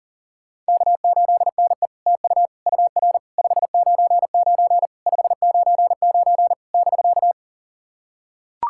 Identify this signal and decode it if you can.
Morse code, heard as a keyed tone, about 40 wpm, 690 Hz. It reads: K8NE TU UR 599 599 <BK>
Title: SOTA QSO Practice 40wpm